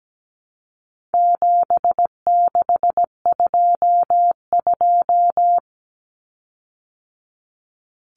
Morse code, heard as a keyed tone, about 17 wpm, 700 Hz. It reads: 7622
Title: Common Cut Numbers - 4-Digit 17wpm